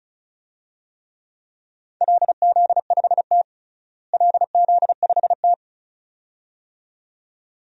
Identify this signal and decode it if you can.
Morse code, heard as a keyed tone, about 36 wpm, 700 Hz. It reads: LZ5T LZ5T
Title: DXCC-100 Call Signs - 4 Characters - Repeated 2x - 35wpm